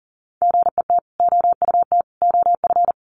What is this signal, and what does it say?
Morse code, heard as a keyed tone, about 40 wpm, 700 Hz. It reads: GET OUT OF
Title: Sets of 3 Words 40wpm